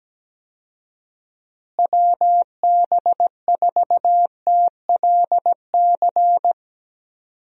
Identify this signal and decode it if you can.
Morse code, heard as a keyed tone, about 17 wpm, 700 Hz. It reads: WB4TLC
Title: USA Call Signs - 6 Characters - Repeated 2x - 17wpm